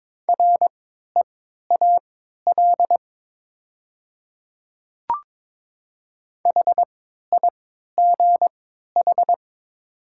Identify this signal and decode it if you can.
Morse code, heard as a keyed tone, about 22 wpm, spaced as for 13 wpm, 700 Hz. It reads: REAL HIGH
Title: 3x Character Spacing - Top 500 Words - 22wpm